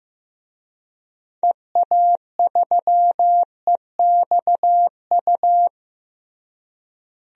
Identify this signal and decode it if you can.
Morse code, heard as a keyed tone, about 15 wpm, 700 Hz. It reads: EA3EXU
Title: DXCC-100 Call Signs - 6 Characters - Repeated 2x - 15wpm